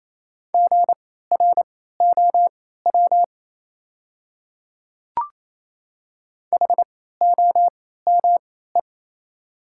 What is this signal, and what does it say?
Morse code, heard as a keyed tone, about 27 wpm, spaced as for 16 wpm, 690 Hz. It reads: GROW HOME
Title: 3x Character Spacing - Top 500 Words - 28wpm